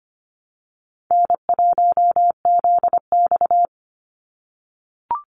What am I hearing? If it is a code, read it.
Morse code, heard as a keyed tone, about 25 wpm, 700 Hz. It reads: N1ZX
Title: Easy Call Signs - 25wpm